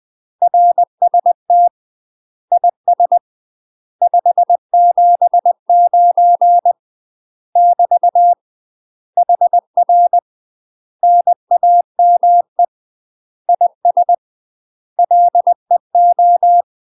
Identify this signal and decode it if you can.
Morse code, heard as a keyed tone, about 20 wpm, 700 Hz. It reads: RST IS 579 <BT> HR NAME IS LEO
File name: UK MASTER QSOs TRAINING / 20WPM / UK MASTER QSOs 0053